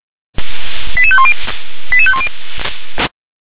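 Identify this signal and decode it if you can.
Five-tone selective call (ZVEI), heard as quick runs of five tones, tones of about 70 ms. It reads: 88510 88510
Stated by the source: ZVEI five-tone call received on an SDR.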